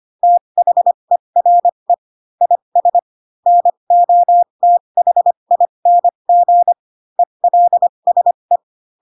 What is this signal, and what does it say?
Morse code, heard as a keyed tone, about 25 wpm, spaced as for 22 wpm, 700 Hz. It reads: THERE IS NOTHING ELSE
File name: Pinocchio / Pinocchio ch7 18 25 ik1hge